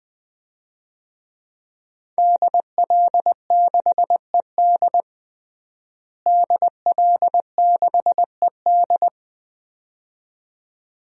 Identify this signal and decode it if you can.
Morse code, heard as a keyed tone, about 21 wpm, 700 Hz. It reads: DL6ED DL6ED
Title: DXCC-100 Call Signs - 5 Characters - Repeated 2x - 20wpm